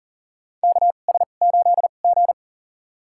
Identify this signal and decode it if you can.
Morse code, heard as a keyed tone, about 40 wpm, spaced as for 29 wpm, 700 Hz. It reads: KS8G
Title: USA Call Signs - 4 Characters - 2x Character Spacing - 40wpm